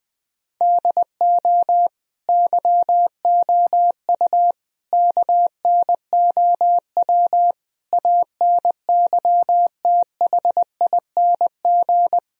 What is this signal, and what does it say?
Morse code, heard as a keyed tone, about 20 wpm, 700 Hz. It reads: DO YOU KNOW ANYTHING
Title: Sentences from Top 1000 Words - Review Part 1 - 20wpm